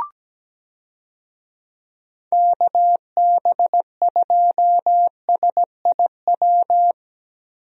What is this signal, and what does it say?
Morse code, heard as a keyed tone, about 17 wpm, 700 Hz. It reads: KB2SIW
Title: USA Call Signs - 6 Characters - Repeated 3x - 17wpm